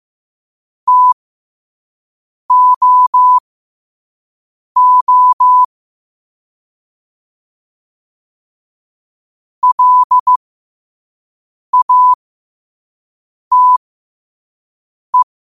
Morse code, heard as a keyed tone, about 15 wpm, spaced as for 5 wpm, 1000 Hz. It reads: TOO LATE
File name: qotdfiles / 5wpm